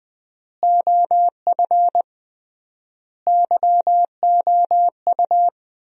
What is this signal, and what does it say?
Morse code, heard as a keyed tone, about 20 wpm, 700 Hz. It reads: OF YOU
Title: 3x Word-Spacing - Sentences from Top 300 Words - 20wpm